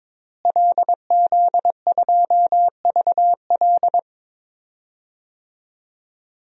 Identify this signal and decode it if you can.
Morse code, heard as a keyed tone, about 21 wpm, 700 Hz. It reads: LZ2VL
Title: DXCC-100 Call Signs - 5 Characters - Repeated 3x - 22wpm